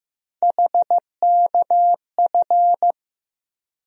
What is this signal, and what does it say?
Morse code, heard as a keyed tone, about 15 wpm, 700 Hz. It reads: HKF
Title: USA Call Sign Postfixes - 3 Letters 15wpm